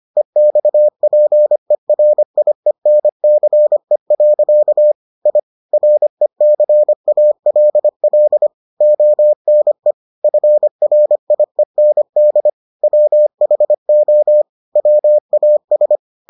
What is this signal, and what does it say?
Morse code, heard as a keyed tone, about 25 wpm, 600 Hz. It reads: EXPERIENCE. I RECALL ONE FRIEND WHO WAS